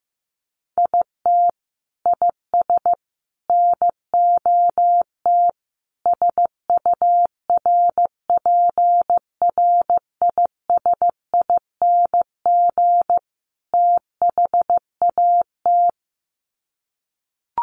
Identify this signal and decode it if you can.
Morse code, heard as a keyed tone, about 15 wpm, 700 Hz. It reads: IT IS NOT SURPRISING THAT
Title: Sets of 5 Words 15wpm